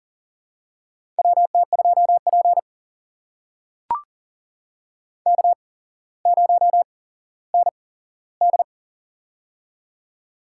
Morse code, heard as a keyed tone, about 40 wpm, spaced as for 17 wpm, 700 Hz. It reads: WT2P K0ND
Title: USA Call Signs - 4 Characters - 8x Character Spacing - 40wpm